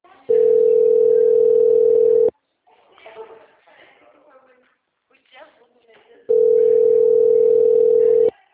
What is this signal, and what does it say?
Ringback tone (North America), 440 + 480 Hz, with speech.
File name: out- -2026- - - .46708